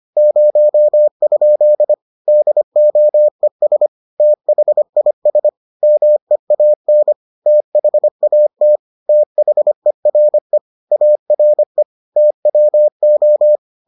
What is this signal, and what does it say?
Morse code, heard as a keyed tone, about 25 wpm, 600 Hz. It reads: 0? DOES THIS MEAN THAT THERE ARE TWO